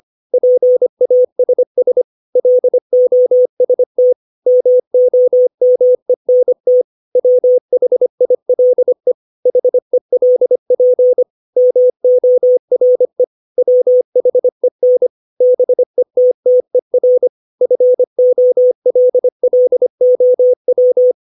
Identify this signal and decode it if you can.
Morse code, heard as a keyed tone, about 25 wpm, 500 Hz. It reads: PASS LOST MOMENT WHILE HELP MORE WHEN BETTER FOLLOW
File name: 632 words / 632 Words 25WPM